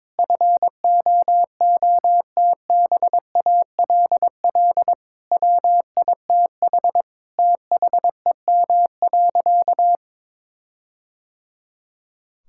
Morse code, heard as a keyed tone, about 22 wpm, 700 Hz. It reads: FOOTBALL WITH THEM.
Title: Sentences from Top 1600 Words 22wpm